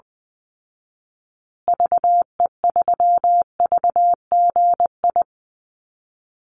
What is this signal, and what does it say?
Morse code, heard as a keyed tone, about 20 wpm, 700 Hz. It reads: VE3VGI
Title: Call Signs Repeated 2x - 20wpm